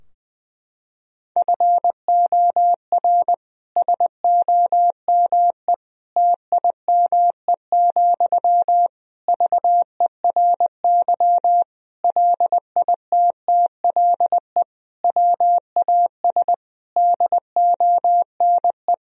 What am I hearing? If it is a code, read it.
Morse code, heard as a keyed tone, about 20 wpm, 700 Hz. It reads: FOR SOME TIME, VERY LITTLE WAS DONE
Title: Sentences from Top 1000 Words - Review Part 4 - 20wpm